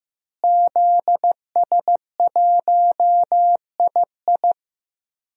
Morse code, heard as a keyed tone, about 15 wpm, 700 Hz. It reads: ZS1II
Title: DXCC-100 Call Signs - 5 Characters - Repeated 3x - 15wpm